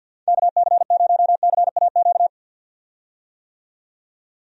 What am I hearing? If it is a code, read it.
Morse code, heard as a keyed tone, about 50 wpm, 700 Hz. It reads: KC0CAQ
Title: USA Call Signs - 6 Characters - 50wpm